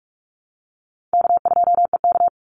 Morse code, heard as a keyed tone, about 44 wpm, 700 Hz. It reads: K2EK
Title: Easy Call Signs Repeated 2x - 45wpm